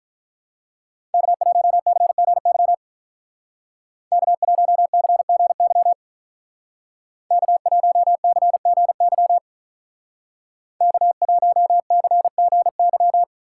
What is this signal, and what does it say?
Morse code, heard as a keyed tone, about 42 wpm, 700 Hz. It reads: K1CGY K1CGY K1CGY K1CGY
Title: Speed Racing - USA 1x3 Call Signs 35wpm